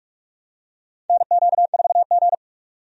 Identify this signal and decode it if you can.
Morse code, heard as a keyed tone, about 45 wpm, 700 Hz. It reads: NQ4G
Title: USA Call Signs - 4 Characters - 45wpm